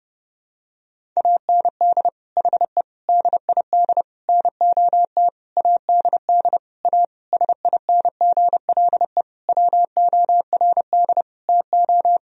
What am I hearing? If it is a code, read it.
Morse code, heard as a keyed tone, about 30 wpm, 700 Hz. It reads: AND HE DID NOT ADD A SINGLE WORD TO
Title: Sentences from Top 1000 Words - Review Part 1 - 30wpm